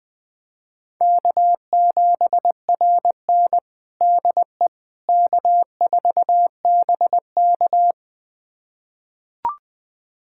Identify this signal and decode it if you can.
Morse code, heard as a keyed tone, about 21 wpm, 700 Hz. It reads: K7RN DE K4BK
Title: USA Callsign DE Callsign - 4 Characters - 20wpm